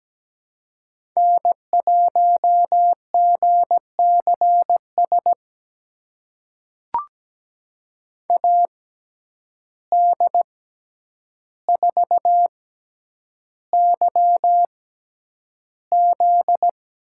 Morse code, heard as a keyed tone, about 17 wpm, spaced as for 9 wpm, 700 Hz. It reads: N1GCS AD4YZ
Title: USA Call Signs - 5 Characters - 6x Character Spacing - 17wpm